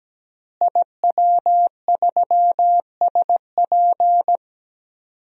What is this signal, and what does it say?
Morse code, heard as a keyed tone, about 17 wpm, 700 Hz. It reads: IW3SP
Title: DXCC-100 Call Signs - 5 Characters - Repeated 2x - 17wpm